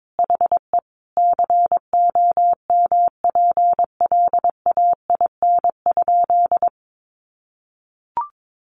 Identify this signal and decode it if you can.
Morse code, heard as a keyed tone, about 22 wpm, 700 Hz. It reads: HE COMPLAIN?
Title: Sentences from Top 1300 Words 22wpm